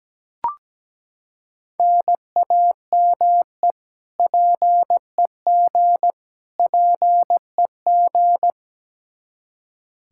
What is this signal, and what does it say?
Morse code, heard as a keyed tone, about 17 wpm, 700 Hz. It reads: NAME PEG PEG
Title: Practice QSO - Exchange 1 - Beginning - 17wpm